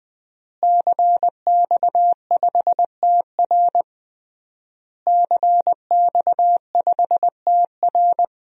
Morse code, heard as a keyed tone, about 21 wpm, 700 Hz. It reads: CX5TR CX5TR
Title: DXCC-100 Call Signs - 5 Characters - Repeated 3x - 20wpm